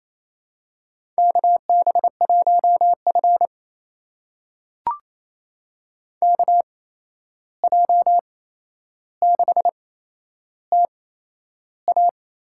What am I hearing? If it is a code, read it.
Morse code, heard as a keyed tone, about 28 wpm, spaced as for 11 wpm, 700 Hz. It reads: KB1F KJ6TA
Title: USA Call Signs - 8x Character Spacing - 28wpm